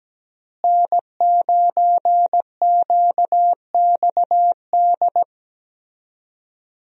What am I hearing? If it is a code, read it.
Morse code, heard as a keyed tone, about 17 wpm, 700 Hz. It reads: N9QXD
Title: USA Call Signs - 5 Characters - Repeated 2x - 17wpm